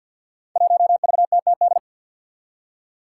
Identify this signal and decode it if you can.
Morse code, heard as a keyed tone, about 47 wpm, 700 Hz. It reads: 1VTTD
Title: Five Challenging Letter-Number Combinations 50wpm